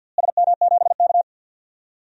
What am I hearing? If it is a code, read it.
Morse code, heard as a keyed tone, about 47 wpm, 700 Hz. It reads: SM7K
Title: DXCC-100 Call Signs - 4 Characters - Repeated 3x - 50wpm